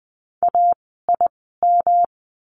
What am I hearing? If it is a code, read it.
Morse code, heard as a keyed tone, about 20 wpm, spaced as for 14 wpm, 700 Hz. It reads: AIM
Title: 108 Lesson 18f - Letter m - Words 20wpm